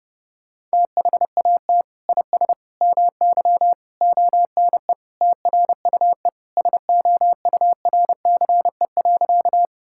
Morse code, heard as a keyed tone, about 29 wpm, 700 Hz. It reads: THAT IS MY ONE TRUE SOURCE.
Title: Sentences from Top 1000 Words - Review Part 5 - 30wpm